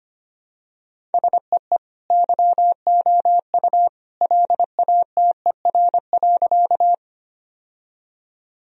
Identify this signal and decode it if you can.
Morse code, heard as a keyed tone, about 25 wpm, 700 Hz. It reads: SEE YOU LATER.
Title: Sentences from Top 1000 Words - Review Part 4 - 25wpm